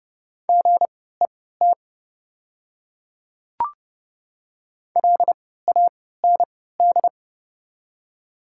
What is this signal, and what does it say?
Morse code, heard as a keyed tone, about 30 wpm, spaced as for 17 wpm, 700 Hz. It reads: GET LAND
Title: 3x Character Spacing - Top 200 Words - 30wpm